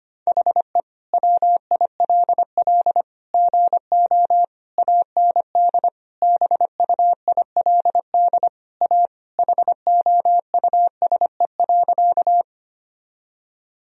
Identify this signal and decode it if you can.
Morse code, heard as a keyed tone, about 25 wpm, 700 Hz. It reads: HE WILL GO AND BUILD A HOUSE.
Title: Sentences from Top 1000 Words - Review Part 1 - 25wpm